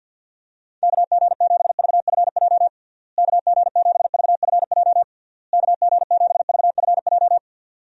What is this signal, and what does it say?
Morse code, heard as a keyed tone, about 51 wpm, 700 Hz. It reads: KG7VFJ KG7VFJ KG7VFJ
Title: USA Call Signs - 6 Characters - Repeated 3x - 50wpm